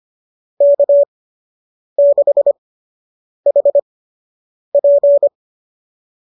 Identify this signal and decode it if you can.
Morse code, heard as a keyed tone, about 25 wpm, spaced as for 8 wpm, 590 Hz. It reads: K6HP
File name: qso003 08